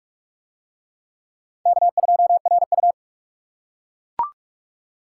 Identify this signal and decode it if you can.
Morse code, heard as a keyed tone, about 42 wpm, 700 Hz. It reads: K2RU